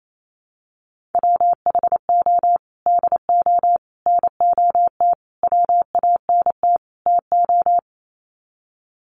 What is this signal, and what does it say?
Morse code, heard as a keyed tone, about 28 wpm, 700 Hz. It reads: WHO DO NOT WANT TO
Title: Sets of 5 Words Encore2 28wpm